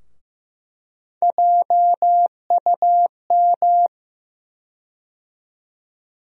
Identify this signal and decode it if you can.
Morse code, heard as a keyed tone, about 15 wpm, 700 Hz. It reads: JUM
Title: USA Call Sign Postfixes - 3 Letters 15wpm